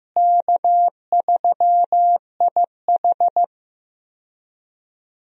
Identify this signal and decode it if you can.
Morse code, heard as a keyed tone, about 15 wpm, 700 Hz. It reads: K3IH
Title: USA Call Signs - 4 Characters - Repeated 3x - 15wpm